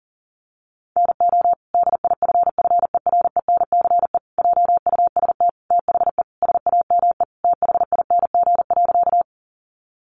Morse code, heard as a keyed tone, about 40 wpm, 700 Hz. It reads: NO DIFFERENCE JUST THE SAME THING.
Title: Sentences from Top 800 Words 40wpm